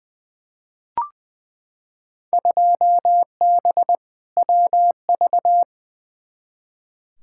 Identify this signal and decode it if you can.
Morse code, heard as a keyed tone, about 20 wpm, 700 Hz. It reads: 2B WV
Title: Field Day Exchanges Encore - 20wpm